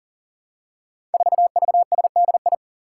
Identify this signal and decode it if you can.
Morse code, heard as a keyed tone, about 40 wpm, 700 Hz. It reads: 4VSDI
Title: Five Challenging Letter-Number Combinations 40wpm